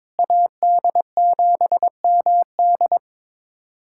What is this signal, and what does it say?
Morse code, heard as a keyed tone, about 22 wpm, 700 Hz. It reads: AD7MD
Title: USA Callsign DE Callsign - 5 Characters - 22wpm